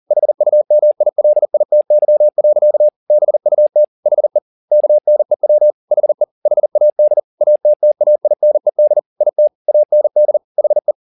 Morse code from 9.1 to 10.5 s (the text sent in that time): IT AND